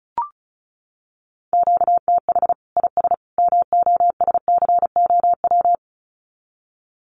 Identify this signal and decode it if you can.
Morse code, heard as a keyed tone, about 34 wpm, 700 Hz. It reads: QTH IS MOSCOW
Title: Common QSO Phrases 35wpm